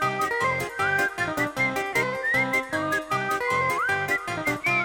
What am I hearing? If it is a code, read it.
Emergency Alert System (SAME) tones, heard as harsh buzzing bursts. It is heard from 0.8 to 4.2 s.